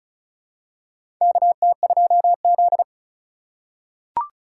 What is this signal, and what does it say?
Morse code, heard as a keyed tone, about 34 wpm, 700 Hz. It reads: KT2Z
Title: USA Call Signs - 4 Characters - 4x Character Spacing - 35wpm